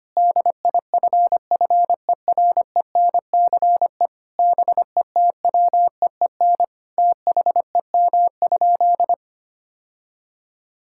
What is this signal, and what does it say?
Morse code, heard as a keyed tone, about 25 wpm, 700 Hz. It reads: DIFFERENCE BETWEEN THEM?
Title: Sentences from Top 1000 Words - Review Part 5 - 25wpm